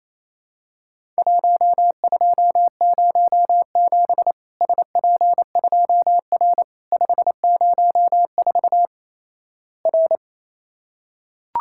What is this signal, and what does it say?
Morse code, heard as a keyed tone, about 28 wpm, 700 Hz. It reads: 1207 SP2R 504 R
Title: Worked All Europe DX Contest - QTC Exchange - 28wpm